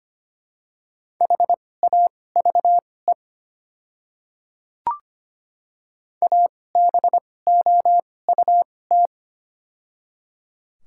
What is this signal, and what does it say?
Morse code, heard as a keyed tone, about 25 wpm, spaced as for 18 wpm, 700 Hz. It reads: HAVE ABOUT
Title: 2x Character Spacing - Top 100 Words - 25wpm